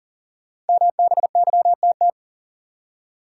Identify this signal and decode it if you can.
Morse code, heard as a keyed tone, about 39 wpm, 700 Hz. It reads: MBYTT
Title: Code Groups - 5 Letter:Numbers - 40wpm